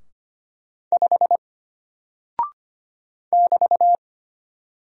Morse code, heard as a keyed tone, about 25 wpm, 700 Hz. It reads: 5 <BT>
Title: Single Character (Letter:Number:Punctuation:Prosign) Alt 25wpm